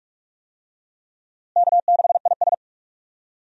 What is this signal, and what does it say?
Morse code, heard as a keyed tone, about 46 wpm, 700 Hz. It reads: K6IS
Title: USA Call Signs - 4 Characters - Repeated 3x - 45wpm